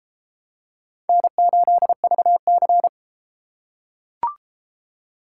Morse code, heard as a keyed tone, about 31 wpm, 700 Hz. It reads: N8VC